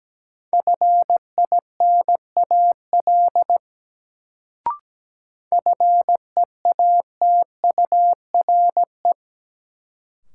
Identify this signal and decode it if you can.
Morse code, heard as a keyed tone, about 17 wpm, 700 Hz. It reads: FINAL FEATURE
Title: Sentences from Top 800 Words - New Word Drill - 17wpm